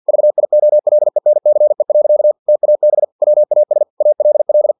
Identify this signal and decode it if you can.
Morse code, heard as a keyed tone, about 50 wpm, 600 Hz. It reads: VIOLENCE. TAB WAS ALL
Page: Never 49wpm